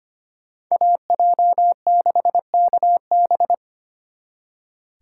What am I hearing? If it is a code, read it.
Morse code, heard as a keyed tone, about 33 wpm, 700 Hz. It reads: AJ6KB